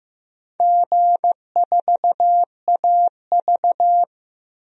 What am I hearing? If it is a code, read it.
Morse code, heard as a keyed tone, about 15 wpm, 700 Hz. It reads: G4AV
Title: DXCC-100 Call Signs - 4 Characters - 15wpm